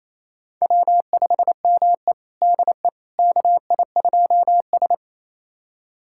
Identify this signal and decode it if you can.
Morse code, heard as a keyed tone, about 28 wpm, 690 Hz. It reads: W5ME DE KI2S
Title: USA Callsign DE Callsign - 4 Characters - 28wpm